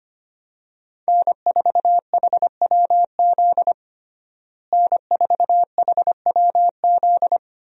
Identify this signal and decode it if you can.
Morse code, heard as a keyed tone, about 25 wpm, 700 Hz. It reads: N4HWZ N4HWZ
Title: USA Call Signs - 5 Characters - Repeated 2x - 25wpm